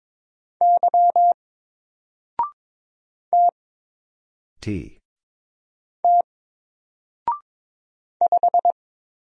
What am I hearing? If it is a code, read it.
Morse code, heard as a keyed tone, about 22 wpm, 700 Hz. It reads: Y T T 5
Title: Single Letter-Number 22wpm